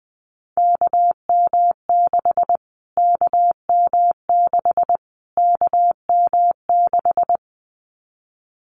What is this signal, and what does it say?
Morse code, heard as a keyed tone, about 20 wpm, 700 Hz. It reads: KM6 KM6 KM6
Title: Callsign Prefixes for Top 25 DXCC - Repeated 3x - 20wpm